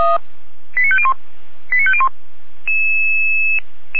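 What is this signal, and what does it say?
Five-tone selective call (ZVEI), heard as quick runs of five tones, tones of about 70 ms. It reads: 89501 89501
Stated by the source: ZVEI five-tone selective call 89108 received by SDR, from Feuerwehr Orlamünde.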